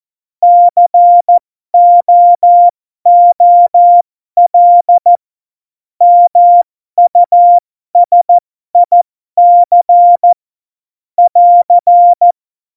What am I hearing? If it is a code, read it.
Morse code, heard as a keyed tone, about 14 wpm, spaced as for 12 wpm, 700 Hz. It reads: COOL MUSIC <AR>
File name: Morsekurs / Witze